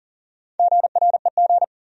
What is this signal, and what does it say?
Morse code, heard as a keyed tone, about 40 wpm, 700 Hz. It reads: GREG
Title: USA Common Names - Max 4 Letters - 40wpm